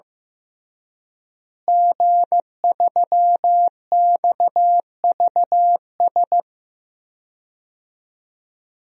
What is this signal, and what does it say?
Morse code, heard as a keyed tone, about 16 wpm, 700 Hz. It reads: G3XVS
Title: DXCC-100 Call Signs - 5 Characters - Repeated 2x - 15wpm